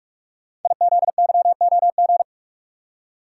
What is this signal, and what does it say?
Morse code, heard as a keyed tone, about 44 wpm, 700 Hz. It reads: IZYOG
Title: Code Groups - 5 Letters - 45wpm